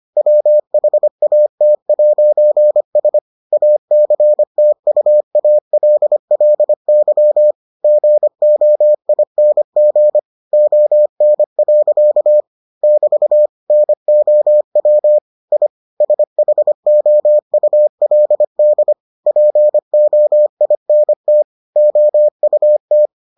Morse code, heard as a keyed tone, about 25 wpm, 600 Hz. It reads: WHAT'S ACTUALLY GOING ON. <BT> NOW I SHOULD POINT OUT